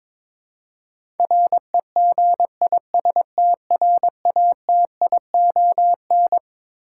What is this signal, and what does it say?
Morse code, heard as a keyed tone, about 22 wpm, 700 Hz. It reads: REGISTRATION
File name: practice / 22wpm / Twelve Letter Words - 22wpm